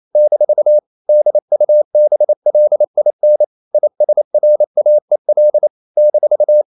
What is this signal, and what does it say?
Morse code, heard as a keyed tone, about 28 wpm, 600 Hz. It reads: - DUBLIN ISRAEL -
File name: CW / Practice / NationalCapitals28